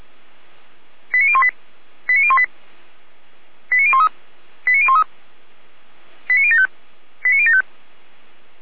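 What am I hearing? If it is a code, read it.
Five-tone selective call (ZVEI), heard as quick runs of five tones, tones of about 70 ms. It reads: 89018 89018 89013 89013 89075 89075
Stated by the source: ZVEI five-tone call, address 89500, received on an SDR by feuerwehr-orlamuende (Orlamünde fire brigade).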